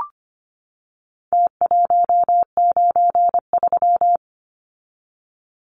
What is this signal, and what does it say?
Morse code, heard as a keyed tone, about 25 wpm, 700 Hz. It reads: T193
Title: Numbers - Contest Style - 4-Digit 25wpm